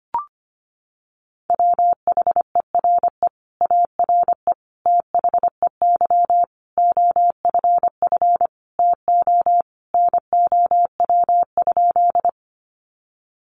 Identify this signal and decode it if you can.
Morse code, heard as a keyed tone, about 25 wpm, 700 Hz. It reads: WHERE ARE THEY OFF TO NOW?
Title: Sentences from Top 200 Words 25wpm